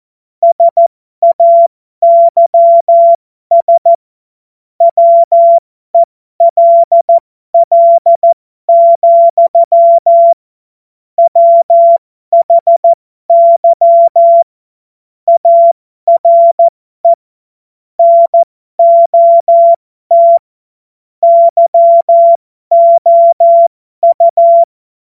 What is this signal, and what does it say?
Morse code, heard as a keyed tone, about 14 wpm, spaced as for 12 wpm, 680 Hz. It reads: SAYS WELL, WHY ARE NOT YOU
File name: Morsekurs / Witze